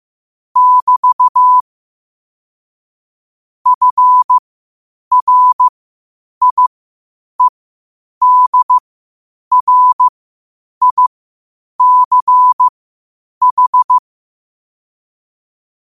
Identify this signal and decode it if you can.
Morse code, heard as a keyed tone, about 15 wpm, spaced as for 8 wpm, 1000 Hz. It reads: <BT> FRIEDRICH
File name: qotdfiles / 8wpm